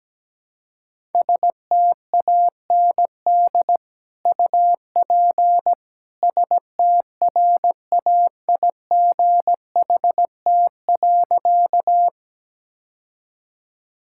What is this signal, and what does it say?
Morse code, heard as a keyed tone, about 17 wpm, 700 Hz. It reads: STAND UP STRAIGHT.
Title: Sentences from Top 1000 Words - Review Part 6 - 17wpm